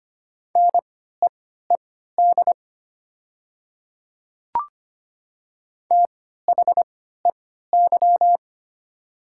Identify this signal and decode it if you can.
Morse code, heard as a keyed tone, about 25 wpm, spaced as for 14 wpm, 700 Hz. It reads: NEED THEY
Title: 3x Character Spacing - Top 300 Words - 25wpm